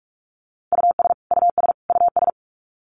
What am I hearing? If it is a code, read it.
Morse code, heard as a keyed tone, about 44 wpm, 700 Hz. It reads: US US US
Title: Top 200 Words Repeated 45wpm